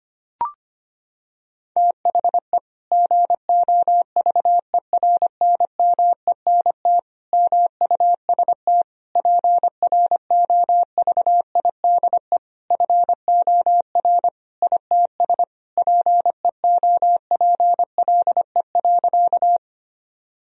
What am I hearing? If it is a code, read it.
Morse code, heard as a keyed tone, about 25 wpm, 700 Hz. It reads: THE GOVERNMENT MUST PROVIDE FOR ITS PEOPLE.
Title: Sentences from Top 1000 Words - Review Part 3 - 25wpm